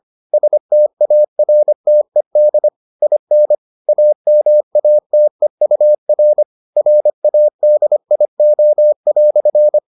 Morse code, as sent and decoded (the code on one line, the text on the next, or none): ... - .- .-. - . -.. / .. -. / .- -- .- - . ..- .-. / .-. .- -.. .. --- .-..-.
STARTED IN AMATEUR RADIO"